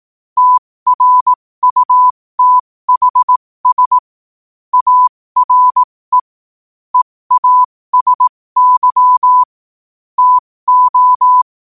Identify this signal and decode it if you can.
Morse code, heard as a keyed tone, about 18 wpm, spaced as for 15 wpm, 1000 Hz. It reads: TRUTHS ARE EASY TO
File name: qotdfiles / 15wpm